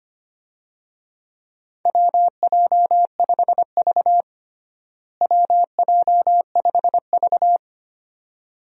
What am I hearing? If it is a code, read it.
Morse code, heard as a keyed tone, about 25 wpm, 700 Hz. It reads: WJ5V WJ5V
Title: USA Call Signs - 4 Characters - Repeated 2x - 25wpm